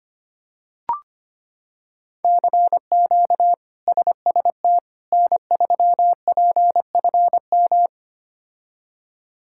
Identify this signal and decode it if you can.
Morse code, heard as a keyed tone, about 25 wpm, 700 Hz. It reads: CQ SST N3PFM
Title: K1USN SST QSO Practice - 25wpm